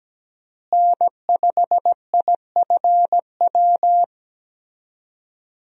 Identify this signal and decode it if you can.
Morse code, heard as a keyed tone, about 17 wpm, 700 Hz. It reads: N5IFW
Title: USA Call Signs - 5 Characters - Repeated 2x - 17wpm